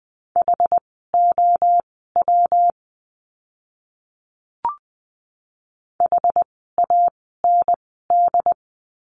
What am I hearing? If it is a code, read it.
Morse code, heard as a keyed tone, about 20 wpm, spaced as for 14 wpm, 700 Hz. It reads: HOW HAND
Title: 244 Lesson 39f - Letter z - Words 20wpm